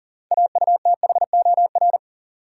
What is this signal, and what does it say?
Morse code, heard as a keyed tone, about 40 wpm, 700 Hz. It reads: AUTHOR
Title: Sets of 4 Words - New Word Drill - 40wpm